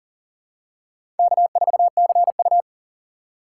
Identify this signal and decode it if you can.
Morse code, heard as a keyed tone, about 40 wpm, 700 Hz. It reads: K4CU
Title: USA Call Signs - 4 Characters - 2x Character Spacing - 40wpm